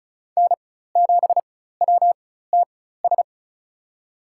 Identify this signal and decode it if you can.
Morse code, heard as a keyed tone, about 36 wpm, spaced as for 16 wpm, 700 Hz. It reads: N7WTS
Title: USA Call Signs - 4x Character Spacing - 35wpm